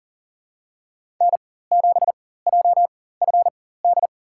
Morse code, heard as a keyed tone, about 40 wpm, spaced as for 19 wpm, 700 Hz. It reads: N7JFD
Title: USA Call Signs - 5 Characters - 4x Character Spacing - 40wpm